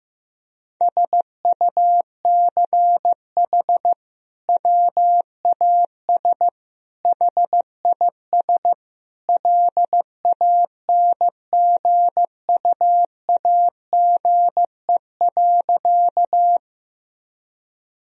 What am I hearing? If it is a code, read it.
Morse code, heard as a keyed tone, about 15 wpm, 700 Hz. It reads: SUCH WAS HIS LANGUAGE.
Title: Sentences from Top 1000 Words - Review Part 3 - 15wpm